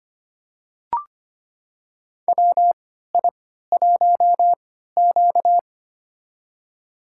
Morse code, heard as a keyed tone, about 25 wpm, spaced as for 14 wpm, 700 Hz. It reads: WI1Q
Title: USA Call Signs - 4 Characters - 3x Character Spacing - 25wpm